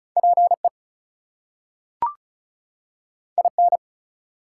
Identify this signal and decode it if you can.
Morse code, heard as a keyed tone, about 36 wpm, 700 Hz. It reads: PE IN
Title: US States and Canadian Provinces 35wpm